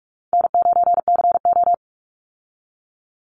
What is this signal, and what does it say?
Morse code, heard as a keyed tone, about 44 wpm, 700 Hz. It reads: N9CO
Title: Call Signs Repeated 2x - 45wpm